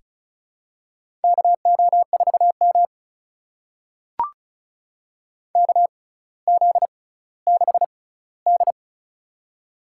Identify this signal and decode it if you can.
Morse code, heard as a keyed tone, about 36 wpm, spaced as for 18 wpm, 700 Hz. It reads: KO4M KZ6D
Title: USA Call Signs - 4 Characters - 6x Character Spacing - 35wpm